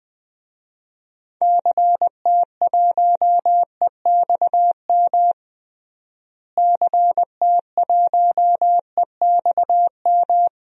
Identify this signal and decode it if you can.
Morse code, heard as a keyed tone, about 20 wpm, 700 Hz. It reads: CT1EXM CT1EXM
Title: DXCC-100 Call Signs - 6 Characters - Repeated 2x - 20wpm